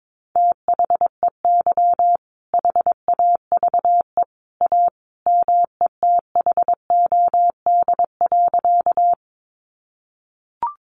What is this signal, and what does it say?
Morse code, heard as a keyed tone, about 22 wpm, 700 Hz. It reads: THEY HAVE A METHOD.
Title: Sentences from Top 600 Words 22wpm